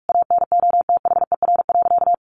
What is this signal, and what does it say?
Morse code, heard as a keyed tone, about 44 wpm, 700 Hz. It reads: ANOTHER.
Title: Sentences from Top 1400 Words 45wpm